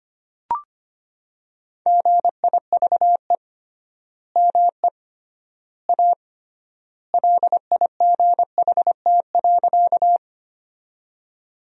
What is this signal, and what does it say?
Morse code, heard as a keyed tone, about 25 wpm, 700 Hz. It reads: GIVE ME A LIGHT.
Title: 3x Word-Spacing - Sentences from Top 200 Words - 25wpm